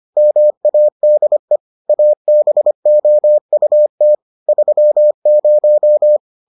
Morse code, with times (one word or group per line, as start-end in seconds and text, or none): 0.2-1.6 MADE
1.9-4.1 ABOUT
4.5-6.2 30